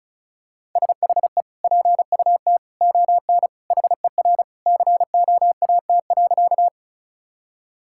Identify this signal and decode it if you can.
Morse code, heard as a keyed tone, about 34 wpm, 700 Hz. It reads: SHE PUT ON HER COAT.
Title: Sentences from Top 1600 Words 35wpm